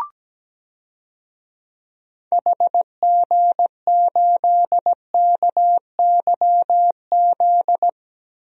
Morse code, heard as a keyed tone, about 17 wpm, 700 Hz. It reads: HG8KYZ
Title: DXCC-100 Call Signs - 6 Characters - Repeated 2x - 17wpm